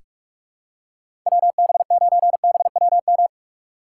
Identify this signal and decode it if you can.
Morse code, heard as a keyed tone, about 46 wpm, 700 Hz. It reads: WB9BWM
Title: USA Call Signs - 6 Characters - 4x Character Spacing - 45wpm